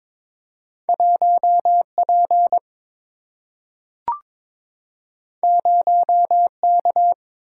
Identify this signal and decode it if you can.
Morse code, heard as a keyed tone, about 22 wpm, 700 Hz. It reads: 1P 0K
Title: Single-Letter Single-Number 22wpm